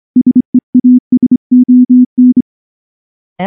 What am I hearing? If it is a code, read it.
Morse code, heard as a keyed tone, about 24 wpm, 250 Hz. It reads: SEASON
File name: morse / morse-training-25WPM